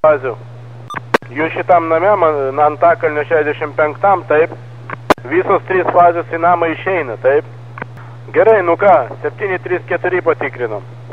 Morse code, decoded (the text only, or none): none